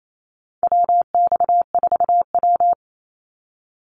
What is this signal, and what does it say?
Morse code, heard as a keyed tone, about 28 wpm, 700 Hz. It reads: WX4W
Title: Call Signs Repeated 2x - 28wpm